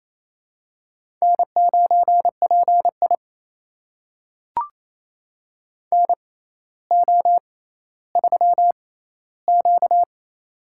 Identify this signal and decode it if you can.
Morse code, heard as a keyed tone, about 27 wpm, spaced as for 14 wpm, 700 Hz. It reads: N9PI NO3Q
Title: USA Call Signs - 6x Character Spacing - 28wpm